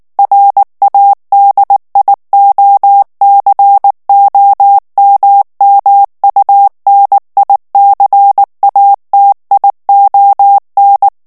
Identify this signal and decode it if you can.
Morse code, heard as a keyed tone, about 19 wpm, 800 Hz. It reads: RADIOCOMMUNICATION
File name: ANGLAIS texte clair 3 vitesse 18 mots minute